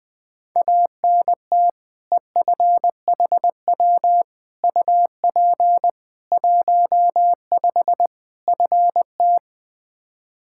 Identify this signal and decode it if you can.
Morse code, heard as a keyed tone, about 20 wpm, 700 Hz. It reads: ANT EFHW UP 15 FT
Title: Practice QSO - Exchange 2 - Info - 20wpm